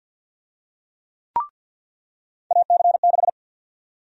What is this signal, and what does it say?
Morse code, heard as a keyed tone, about 46 wpm, 700 Hz. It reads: AC6
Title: USA Call Sign Prefixes - 50wpm